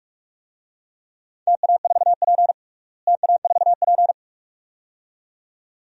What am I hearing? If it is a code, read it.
Morse code, heard as a keyed tone, about 42 wpm, 700 Hz. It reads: TA4P TA4P
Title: DXCC-100 Call Signs - 4 Characters - Repeated 2x - 45wpm